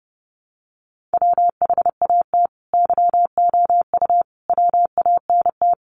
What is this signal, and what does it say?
Morse code, heard as a keyed tone, about 30 wpm, 700 Hz. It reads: WHAT YOU WANT
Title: Sets of 5 Words Encore 30wpm